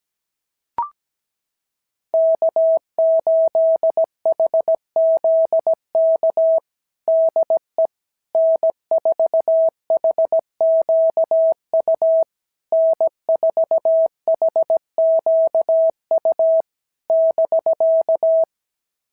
Morse code, heard as a keyed tone, about 17 wpm, 650 Hz. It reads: K8HZK DE N4HQU N4HQU <BK>
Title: SOTA QSO Practice 17wpm